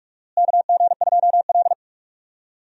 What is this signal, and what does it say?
Morse code, heard as a keyed tone, about 44 wpm, 700 Hz. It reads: KG2L
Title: USA Call Signs - 4 Characters - Repeated 2x - 45wpm